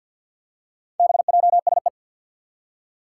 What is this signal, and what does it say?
Morse code, heard as a keyed tone, about 52 wpm, 700 Hz. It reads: BJSE
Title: Four Challenging Letter-Number Combinations 50wpm